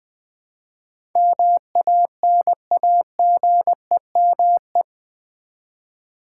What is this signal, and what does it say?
Morse code, heard as a keyed tone, about 20 wpm, 700 Hz. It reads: MANAGEME
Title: Build A Word - 4 Letters At A Time - 10-Letter Words - 20wpm